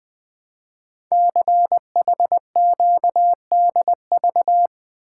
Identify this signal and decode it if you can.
Morse code, heard as a keyed tone, about 20 wpm, 700 Hz. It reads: CHQDV
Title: Code Groups - 5 Letter:Numbers - 20wpm